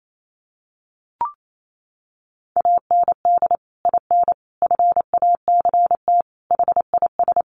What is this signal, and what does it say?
Morse code, heard as a keyed tone, about 28 wpm, 700 Hz. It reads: AND IN FACT HIS
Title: Sentences from Top 1000 Words 28wpm